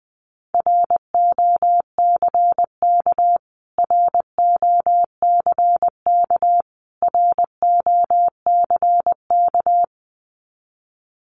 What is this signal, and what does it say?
Morse code, heard as a keyed tone, about 20 wpm, 700 Hz. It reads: ROCK ROCK ROCK
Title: Top 500 Words Repeated 20wpm